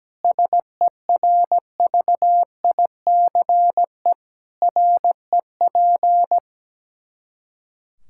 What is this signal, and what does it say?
Morse code, heard as a keyed tone, about 17 wpm, 700 Hz. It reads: SERVICE REP
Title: Practice QSO - Exchange 3 - More Info - 17wpm